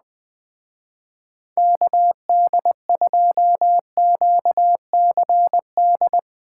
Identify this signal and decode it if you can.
Morse code, heard as a keyed tone, about 20 wpm, 700 Hz. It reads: KD2QCD